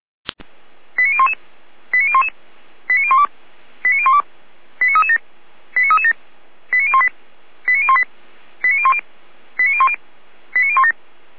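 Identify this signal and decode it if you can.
Five-tone selective call (ZVEI), heard as quick runs of five tones, tones of about 70 ms. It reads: 89011 89011 89012 89012 89337 89337 89018 89018 89010 89010 89017
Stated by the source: ZVEI five-tone selective call received on an SDR.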